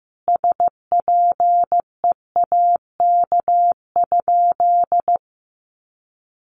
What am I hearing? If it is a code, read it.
Morse code, heard as a keyed tone, about 14 wpm, 700 Hz. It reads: SPEAK?
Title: Sentences from Top 1100 Words 15wpm